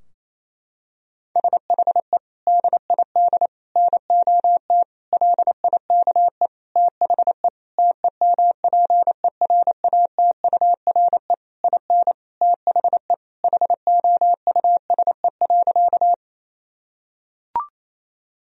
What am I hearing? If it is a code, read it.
Morse code, heard as a keyed tone, about 28 wpm, 700 Hz. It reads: SHE DID NOT LIKE THE TEMPERATURE IN THE HOUSE.
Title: Sentences from Top 1000 Words - Review Part 4 - 28wpm